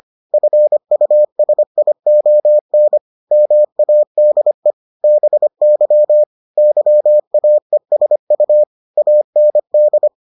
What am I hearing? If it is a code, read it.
Morse code, heard as a keyed tone, about 25 wpm, 600 Hz. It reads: FUSION MADE BY YAESU AND